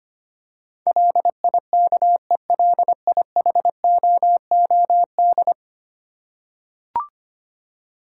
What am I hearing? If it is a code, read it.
Morse code, heard as a keyed tone, about 25 wpm, 700 Hz. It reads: LIKELIHOOD
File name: Ten Letter Words - 25wpm